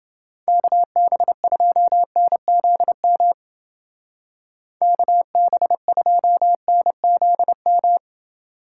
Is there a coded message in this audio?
Yes: Morse code from 0.5 to 8.0 s, heard as a keyed tone, about 29 wpm, 700 Hz. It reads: KB2NZM KB2NZM